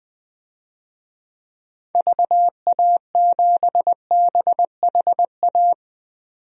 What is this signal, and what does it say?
Morse code, heard as a keyed tone, about 20 wpm, 700 Hz. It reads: VA7BHA
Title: DXCC-100 Call Signs - 6 Characters - 20wpm